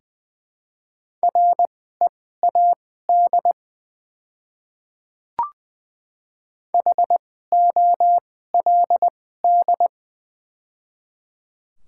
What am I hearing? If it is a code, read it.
Morse code, heard as a keyed tone, about 20 wpm, spaced as for 14 wpm, 700 Hz. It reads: READ HOLD
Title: 2x Character Spacing - Top 500 Words - 20wpm